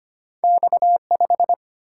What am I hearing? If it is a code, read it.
Morse code, heard as a keyed tone, about 25 wpm, 700 Hz. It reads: X5
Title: Single-Letter Single-Number 25wpm